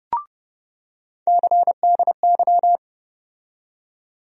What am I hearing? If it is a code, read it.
Morse code, heard as a keyed tone, about 30 wpm, 700 Hz. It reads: CDY
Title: USA Call Sign Postfixes - 3 Letters 30wpm